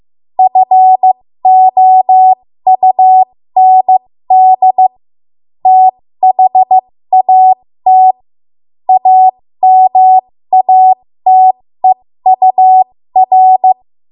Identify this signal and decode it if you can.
Morse code, heard as a keyed tone, about 15 wpm, spaced as for 13 wpm, 750 Hz. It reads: FOUND THAT AMATEUR
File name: Morse / 13WPM